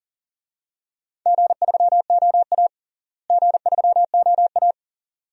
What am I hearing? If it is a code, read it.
Morse code, heard as a keyed tone, about 38 wpm, 700 Hz. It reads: G3OA G3OA
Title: DXCC-100 Call Signs - 4 Characters - Repeated 2x - 40wpm